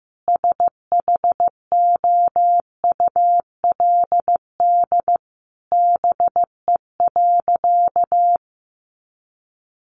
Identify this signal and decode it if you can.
Morse code, heard as a keyed tone, about 15 wpm, 700 Hz. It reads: SHOULD BE.
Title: Sentences from Top 700 Words 15wpm